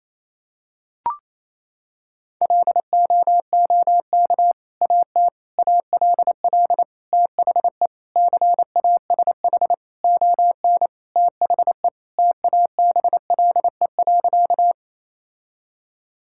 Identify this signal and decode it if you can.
Morse code, heard as a keyed tone, about 28 wpm, 700 Hz. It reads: LOOK AT ALL THE CASH ON THE TABLE.
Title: Sentences from Top 1000 Words - Review Part 5 - 28wpm